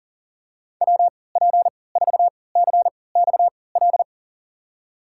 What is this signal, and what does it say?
Morse code, heard as a keyed tone, about 40 wpm, spaced as for 23 wpm, 700 Hz. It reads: WP4CXL
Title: USA Call Signs - 6 Characters - 3x Character Spacing - 40wpm